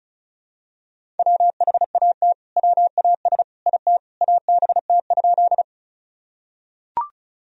Morse code, heard as a keyed tone, about 34 wpm, 700 Hz. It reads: WHAT WAS IT ABT?
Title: Sentences from Top 100 Words and Abbreviations 35wpm